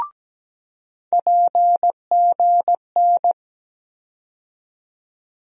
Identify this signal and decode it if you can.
Morse code, heard as a keyed tone, about 17 wpm, 700 Hz. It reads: PGN
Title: USA Call Sign Postfixes - 3 Letters 17wpm